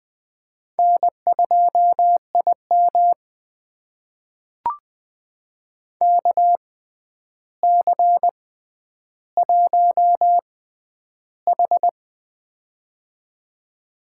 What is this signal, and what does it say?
Morse code, heard as a keyed tone, about 20 wpm, spaced as for 10 wpm, 700 Hz. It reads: N2IM KC1H
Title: USA Call Signs - 4 Characters - 6x Character Spacing - 20wpm